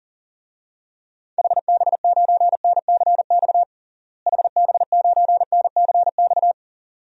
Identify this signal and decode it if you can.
Morse code, heard as a keyed tone, about 40 wpm, 700 Hz. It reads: HB9NCX HB9NCX
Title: DXCC-100 Call Signs - 6 Characters - Repeated 2x - 40wpm